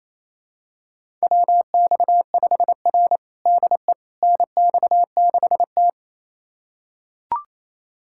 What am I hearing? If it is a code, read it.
Morse code, heard as a keyed tone, about 28 wpm, 700 Hz. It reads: WX5R DE NX6T
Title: USA Callsign DE Callsign - 4 Characters - 28wpm